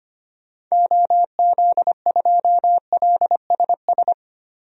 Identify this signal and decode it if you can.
Morse code, heard as a keyed tone, about 25 wpm, 700 Hz. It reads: OZ2LSS
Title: DXCC-100 Call Signs - 6 Characters - Repeated 3x - 25wpm